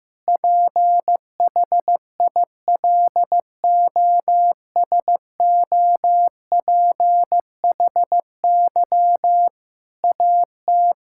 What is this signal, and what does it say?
Morse code, heard as a keyed tone, about 15 wpm, 700 Hz. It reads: PHILOSOPHY AT
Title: Sentences from Top 1000 Words - Review Part 6 - 15wpm